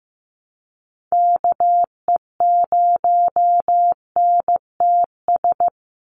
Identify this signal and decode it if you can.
Morse code, heard as a keyed tone, about 15 wpm, 700 Hz. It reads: KE0NTS
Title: USA Call Signs 15wpm